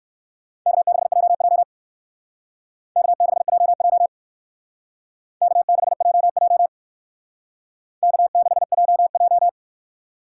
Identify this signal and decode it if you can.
Morse code, heard as a keyed tone, about 56 wpm, 700 Hz. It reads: K6JJ K6JJ K6JJ K6JJ